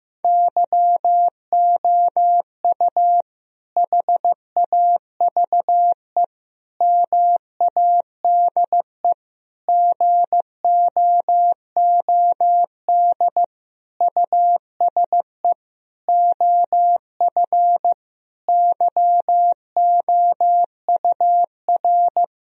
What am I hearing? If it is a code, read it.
Morse code, heard as a keyed tone, about 15 wpm, 700 Hz. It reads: YOU HAVE MADE GOOD USE OF YOUR
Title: Sentences from Top 1000 Words - Review Part 1 - 15wpm